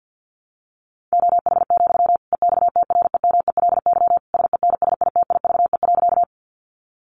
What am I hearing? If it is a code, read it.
Morse code, heard as a keyed tone, about 50 wpm, 700 Hz. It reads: OH, EXTREMELY SENSITIVE.
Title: Sentences from Top 1400 Words 50wpm